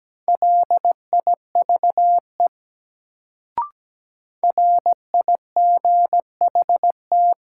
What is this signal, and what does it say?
Morse code, heard as a keyed tone, about 17 wpm, 700 Hz. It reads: LIVE RIGHT
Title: Sentences from Top 200 Words - New Word Drill - 17wpm